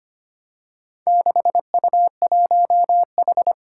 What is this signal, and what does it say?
Morse code, heard as a keyed tone, about 25 wpm, 700 Hz. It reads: 6U1H
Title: Four Challenging Letter-Number Combinations 25wpm